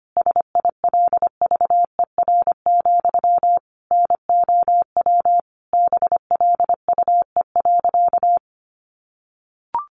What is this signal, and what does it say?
Morse code, heard as a keyed tone, about 25 wpm, 700 Hz. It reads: SILVER, NOW BLUE.
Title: A Sound Of Thunder 25wpm